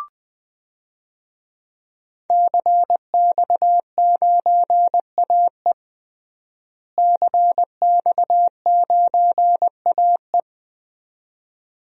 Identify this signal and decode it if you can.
Morse code, heard as a keyed tone, about 20 wpm, 700 Hz. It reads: CX9AE CX9AE
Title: DXCC-100 Call Signs - 5 Characters - Repeated 2x - 20wpm